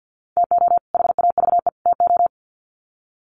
Morse code, heard as a keyed tone, about 53 wpm, 700 Hz. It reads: TO HAVE TO
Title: Sets of 3 Words 50wpm